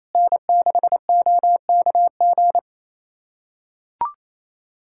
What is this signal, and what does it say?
Morse code, heard as a keyed tone, about 28 wpm, 700 Hz. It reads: N6OKG
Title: USA Callsign DE Callsign - 5 Characters - 28wpm